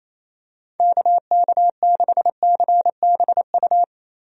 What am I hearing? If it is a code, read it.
Morse code, heard as a keyed tone, about 28 wpm, 700 Hz. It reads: KK6CBU
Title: USA Call Signs - 6 Characters - Repeated 3x - 28wpm